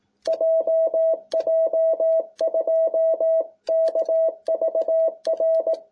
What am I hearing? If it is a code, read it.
Morse code, heard as a keyed tone, about 18 wpm, 660 Hz. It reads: JJ2KVR